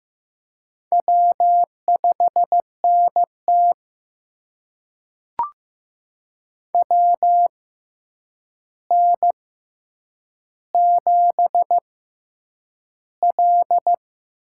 Morse code, heard as a keyed tone, about 15 wpm, spaced as for 8 wpm, 700 Hz. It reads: W5NT WN7L
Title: USA Call Signs - 4 Characters - 6x Character Spacing - 15wpm